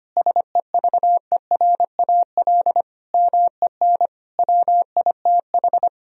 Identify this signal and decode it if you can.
Morse code, heard as a keyed tone, about 25 wpm, 700 Hz. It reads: SEVERAL MEN WITH
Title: Sentences from Top 1000 Words - Review Part 2 - 25wpm